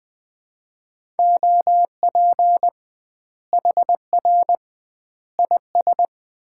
Morse code, heard as a keyed tone, about 21 wpm, 700 Hz. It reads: OP HR IS
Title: Common QSO Phrases - 2x Word-Spacing - 20wpm